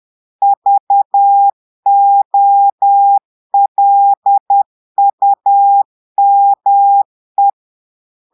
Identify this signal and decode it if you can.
Morse code, heard as a keyed tone, about 10 wpm, 800 Hz. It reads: VOLUME
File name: Portugais - texte en clair 1 - vitesse - 10 mots minute